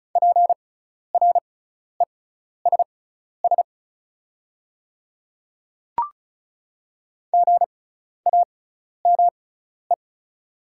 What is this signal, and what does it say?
Morse code, heard as a keyed tone, about 36 wpm, spaced as for 12 wpm, 690 Hz. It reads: PRESS GAME
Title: 6x Character Spacing - Top 500 Words - 35wpm